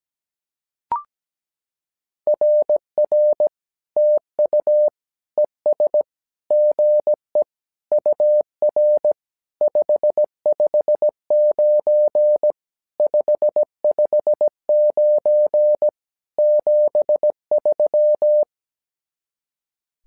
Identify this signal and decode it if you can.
Morse code, heard as a keyed tone, about 17 wpm, 600 Hz. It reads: RR TU ES GE UR 559 559 73
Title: SOTA QSO Practice 17wpm